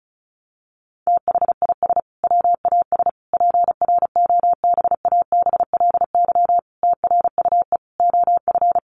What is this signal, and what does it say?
Morse code, heard as a keyed tone, about 34 wpm, 700 Hz. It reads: THIS WAS PROBABLY TRUE OF
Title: Sentences from Top 700 Words 35wpm